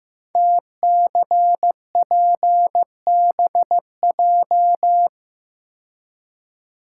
Morse code, heard as a keyed tone, about 15 wpm, 700 Hz. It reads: TCPBJ
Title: Code Groups - 5 Letters - 15wpm